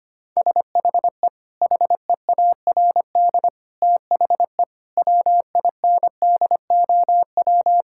Morse code, heard as a keyed tone, about 25 wpm, 700 Hz. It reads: SHE HEARD THE WINDOW
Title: Sentences from Top 1600 Words 25wpm